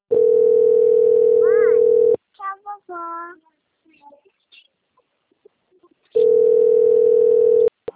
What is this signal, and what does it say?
Ringback tone (North America), 440 + 480 Hz, with speech.
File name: out- - - - - .8870